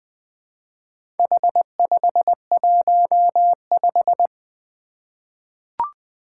Morse code, heard as a keyed tone, about 20 wpm, 700 Hz. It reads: H515
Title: Four Challenging Letter-Number Combinations 20wpm